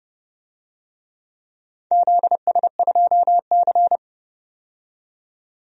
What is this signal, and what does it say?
Morse code, heard as a keyed tone, about 31 wpm, 700 Hz. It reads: ZS2C
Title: DXCC-100 Call Signs - 4 Characters - 30wpm